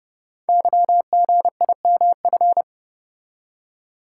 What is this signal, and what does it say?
Morse code, heard as a keyed tone, about 30 wpm, 700 Hz. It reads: YGIMF
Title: Code Groups - 5 Letter:Numbers - 30wpm